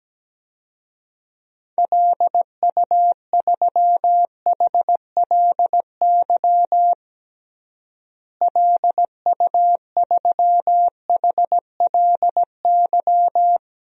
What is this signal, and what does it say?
Morse code, heard as a keyed tone, about 17 wpm, 700 Hz. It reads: LU3HLY LU3HLY
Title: DXCC-100 Call Signs - 6 Characters - Repeated 3x - 17wpm